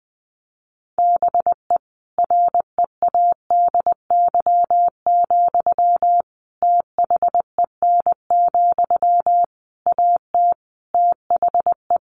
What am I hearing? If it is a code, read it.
Morse code, heard as a keyed tone, about 20 wpm, 700 Hz. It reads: BE READY, THEN, AT THE
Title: Sentences from Top 1400 Words 20wpm